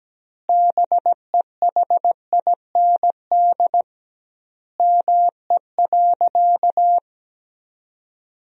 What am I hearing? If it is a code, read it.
Morse code, heard as a keyed tone, about 17 wpm, 700 Hz. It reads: BEHIND ME.
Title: 2x Word-Spacing - Sentences from Top 500 Words - 17wpm